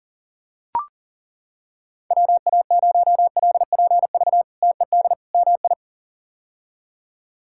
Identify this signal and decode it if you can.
Morse code, heard as a keyed tone, about 37 wpm, 700 Hz. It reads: WA0LPV TED MI
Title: K1USN SST QSO Practice - 40wpm